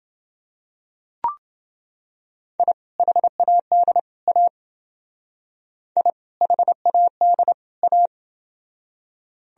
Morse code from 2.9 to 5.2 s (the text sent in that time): HAD A